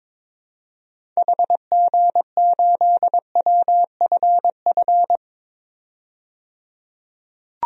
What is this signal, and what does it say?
Morse code, heard as a keyed tone, about 21 wpm, 700 Hz. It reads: HG8WFF
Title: DXCC-100 Call Signs - 6 Characters - 22wpm